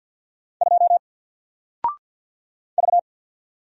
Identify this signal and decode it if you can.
Morse code, heard as a keyed tone, about 48 wpm, 700 Hz. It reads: 2 V
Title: Single Character (Letter:Number:Punctuation:Prosign) 50wpm